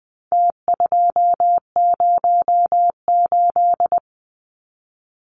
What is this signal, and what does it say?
Morse code, heard as a keyed tone, about 20 wpm, 700 Hz. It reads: T208
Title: Numbers - Contest Style - 4-Digit 20wpm